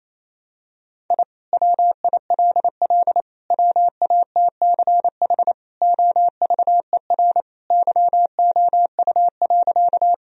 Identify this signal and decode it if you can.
Morse code, heard as a keyed tone, about 28 wpm, 700 Hz. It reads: I WILL WATCH OVER YOU.
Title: Sentences from Top 1000 Words - Review Part 1 - 28wpm